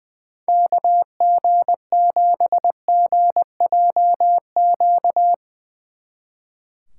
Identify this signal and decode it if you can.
Morse code, heard as a keyed tone, about 22 wpm, 700 Hz. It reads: KG7GJQ
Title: Speed Racing - USA 2x3 Call Signs 20wpm